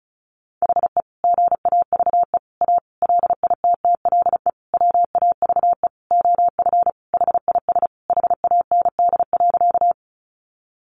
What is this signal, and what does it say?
Morse code, heard as a keyed tone, about 34 wpm, 700 Hz. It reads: HE GAVE A LITTLE WAVE OF HIS HAND.
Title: Sentences from Top 1400 Words 35wpm